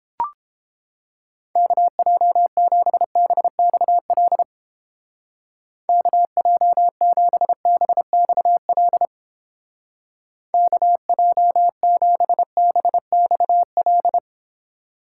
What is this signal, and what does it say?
Morse code, heard as a keyed tone, about 31 wpm, 700 Hz. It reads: KJ7BXL KJ7BXL KJ7BXL